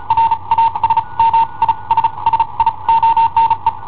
Morse code, heard as a keyed tone, about 35 wpm, spaced as for 28 wpm, 910 Hz. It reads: RASMISSIONE
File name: i7ohp / cassetta i7ohp B